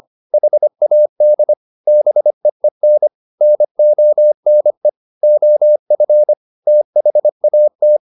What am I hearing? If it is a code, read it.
Morse code, heard as a keyed tone, about 25 wpm, 600 Hz. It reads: HAD BEEN NONE OF THAT